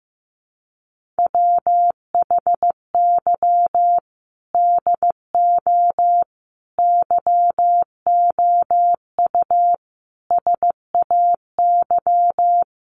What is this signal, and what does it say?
Morse code, heard as a keyed tone, about 15 wpm, 700 Hz. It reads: WHY DO YOU SAY
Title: Sentences from Top 1100 Words 15wpm